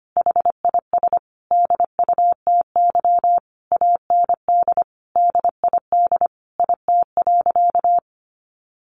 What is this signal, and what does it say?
Morse code, heard as a keyed tone, about 25 wpm, 700 Hz. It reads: HIS DUTY AND DID IT.
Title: Sentences from Top 1500 Words 25wpm